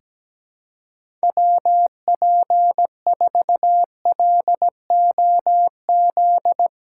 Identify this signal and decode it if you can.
Morse code, heard as a keyed tone, about 17 wpm, 700 Hz. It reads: WP4LOZ
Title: USA Call Signs - 6 Characters - 4x Character Spacing - 17wpm